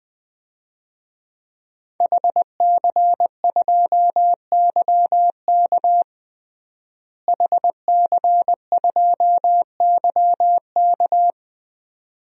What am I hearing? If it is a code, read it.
Morse code, heard as a keyed tone, about 20 wpm, 700 Hz. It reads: HC2YK HC2YK
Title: DXCC-100 Call Signs - 5 Characters - Repeated 2x - 20wpm